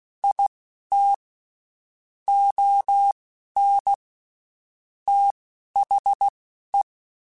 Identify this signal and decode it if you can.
Morse code, heard as a keyed tone, about 16 wpm, spaced as for 11 wpm, 800 Hz. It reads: IT ON THE
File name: Spaced / Chuckles Spaced-mp3s / HumanErr sp @15